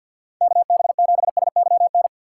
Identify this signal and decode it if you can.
Morse code, heard as a keyed tone, about 49 wpm, 690 Hz. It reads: KB7SYN
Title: USA Call Signs - 6 Characters - 3x Character Spacing - 50wpm